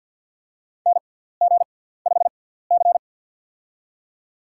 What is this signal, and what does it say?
Morse code, heard as a keyed tone, about 51 wpm, spaced as for 17 wpm, 690 Hz. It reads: NG5C
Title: USA Call Signs - 4 Characters - 6x Character Spacing - 50wpm